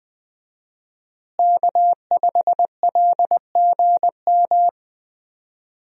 Morse code, heard as a keyed tone, about 20 wpm, 700 Hz. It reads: K5LGM
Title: USA Call Signs - 5 Characters - Repeated 2x - 20wpm